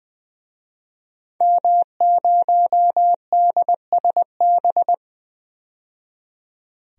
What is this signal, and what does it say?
Morse code, heard as a keyed tone, about 20 wpm, 700 Hz. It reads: M0DSB
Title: DXCC-100 Call Signs - 5 Characters - 20wpm